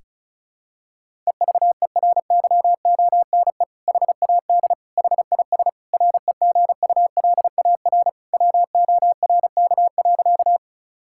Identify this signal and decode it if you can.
Morse code, heard as a keyed tone, about 36 wpm, 700 Hz. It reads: EVERYONE HAD HIS REGULAR WORK.
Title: Sentences from Top 1000 Words - Review Part 6 - 35wpm